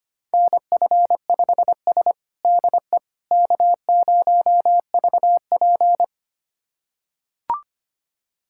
Morse code, heard as a keyed tone, about 25 wpm, 700 Hz. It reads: NF5S DE K0VP
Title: USA Callsign DE Callsign - 4 Characters - 25wpm